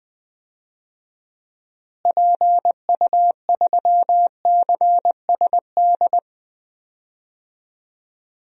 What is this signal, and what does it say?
Morse code, heard as a keyed tone, about 20 wpm, 700 Hz. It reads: PU3CSD
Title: DXCC-100 Call Signs - 6 Characters - Repeated 2x - 20wpm